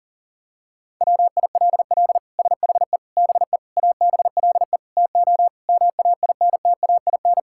Morse code, heard as a keyed tone, about 40 wpm, 700 Hz. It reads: WILL SHE BE ABLE TO MAINTAIN
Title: Sentences from Top 1000 Words - Review Part 5 - 40wpm